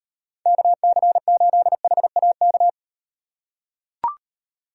Morse code, heard as a keyed tone, about 38 wpm, 700 Hz. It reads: KC8HAK